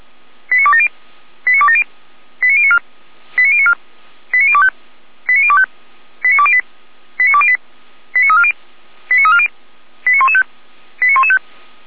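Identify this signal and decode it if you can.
Five-tone selective call (ZVEI), heard as quick runs of five tones, tones of about 80 ms. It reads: 89270 89270 89094 89094 89025 89025 89208 89208 89340 89340 89115 89115
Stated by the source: ZVEI five-tone selective call 89358 received on SDR, from feuerwehr-orlamuende.org fire brigade server.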